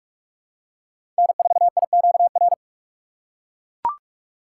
Morse code, heard as a keyed tone, about 43 wpm, 700 Hz. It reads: N4IQR
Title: USA Call Signs - 6x Character Spacing - 45wpm